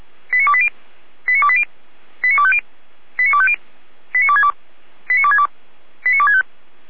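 Five-tone selective call (ZVEI), heard as quick runs of five tones, tones of about 70 ms. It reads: 89280 89280 89250 89250 89262 89262 89265
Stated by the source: ZVEI five-tone selective call received on an SDR.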